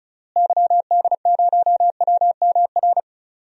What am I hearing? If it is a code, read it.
Morse code, heard as a keyed tone, about 36 wpm, 700 Hz. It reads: YD0WMR
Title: DXCC-100 Call Signs - 6 Characters - Repeated 3x - 35wpm